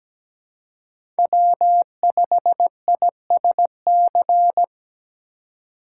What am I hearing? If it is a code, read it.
Morse code, heard as a keyed tone, about 20 wpm, 700 Hz. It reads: W5ISC
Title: Speed Racing - USA 1x3 Call Signs 17wpm